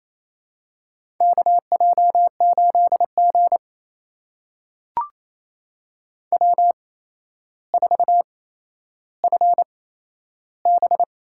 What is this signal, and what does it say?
Morse code, heard as a keyed tone, about 29 wpm, spaced as for 12 wpm, 700 Hz. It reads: KJ8G W4FB
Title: USA Call Signs - 4 Characters - 8x Character Spacing - 28wpm